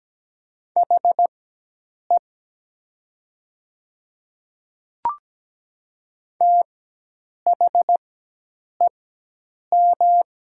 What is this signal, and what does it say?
Morse code, heard as a keyed tone, about 17 wpm, spaced as for 8 wpm, 700 Hz. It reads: HE THEM
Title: 4x Character Spacing - Top 100 Words - 17wpm